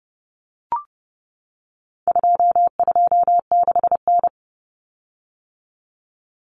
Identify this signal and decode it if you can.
Morse code, heard as a keyed tone, about 30 wpm, 700 Hz. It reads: 226N
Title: Common Cut Numbers - 4-Digit 30wpm